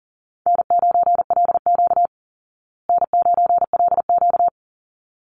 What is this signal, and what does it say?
Morse code, heard as a keyed tone, about 40 wpm, 700 Hz. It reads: N9LQ N9LQ
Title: Easy Call Signs Repeated 2x - 40wpm